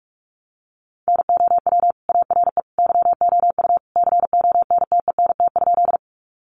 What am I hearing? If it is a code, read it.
Morse code, heard as a keyed tone, about 44 wpm, 700 Hz. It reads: NOW ARE YOU CONTENT?
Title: Sentences from Top 900 Words 45wpm